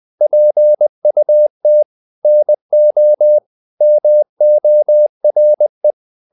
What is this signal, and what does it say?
Morse code, heard as a keyed tone, about 20 wpm, 600 Hz. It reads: PUT NO MORE
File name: the old man and the sea / the old man and the sea part 1 20wpm